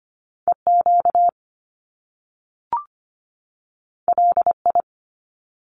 Morse code, heard as a keyed tone, about 25 wpm, 700 Hz. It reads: EQ LI
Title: Top Two Letter Combinations in Words 25wpm